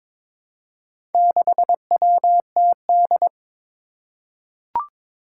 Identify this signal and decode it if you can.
Morse code, heard as a keyed tone, about 22 wpm, 700 Hz. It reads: 6WTD
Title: Four Challenging Letter-Number Combinations 22wpm